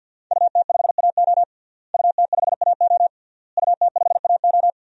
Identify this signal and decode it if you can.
Morse code, heard as a keyed tone, about 51 wpm, 700 Hz. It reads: UT5AO UT5AO UT5AO
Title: DXCC-100 Call Signs - 5 Characters - Repeated 3x - 50wpm